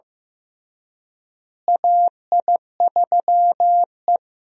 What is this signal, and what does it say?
Morse code, heard as a keyed tone, about 16 wpm, 700 Hz. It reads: AI3E
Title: USA Call Signs - 4 Characters - Repeated 3x - 15wpm